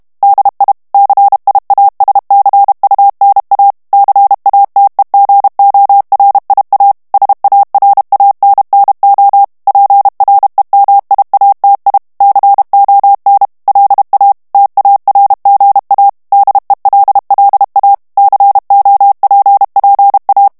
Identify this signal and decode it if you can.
Morse code, heard as a keyed tone, about 32 wpm, 800 Hz. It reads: DI CIASCUNA CATEGORIA SARANNO PREMIATI CON LA TARGA DELLA COPPA
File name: ITALIEN texte clair 2 vitesse 30 mots minute